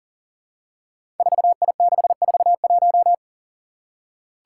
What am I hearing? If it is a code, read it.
Morse code, heard as a keyed tone, about 42 wpm, 700 Hz. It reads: 4I641
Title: Five Challenging Letter-Number Combinations 40wpm